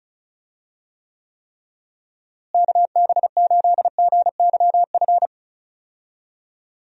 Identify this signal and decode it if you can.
Morse code, heard as a keyed tone, about 36 wpm, 700 Hz. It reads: KB8GYF
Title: USA Call Signs - 6 Characters - Repeated 3x - 35wpm